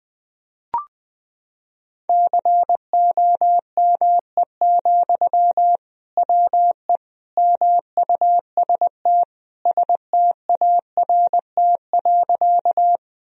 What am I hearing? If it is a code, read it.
Morse code, heard as a keyed tone, about 21 wpm, 700 Hz. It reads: COME, WE MUST START.
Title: Sentences from Top 1000 Words - Review Part 1 - 20wpm